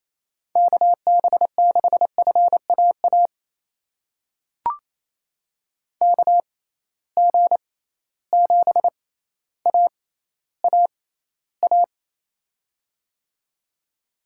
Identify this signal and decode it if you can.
Morse code, heard as a keyed tone, about 28 wpm, spaced as for 14 wpm, 700 Hz. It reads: KB6FAA KG7AAA
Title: USA Call Signs - 6 Characters - 6x Character Spacing - 28wpm